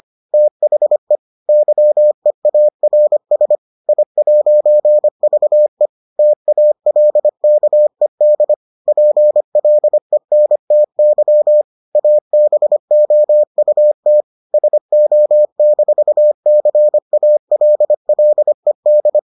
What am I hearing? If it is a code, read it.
Morse code, heard as a keyed tone, about 25 wpm, 600 Hz. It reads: THE YEARS I'VE TALKED PLENTY ABOUT SO-CALLED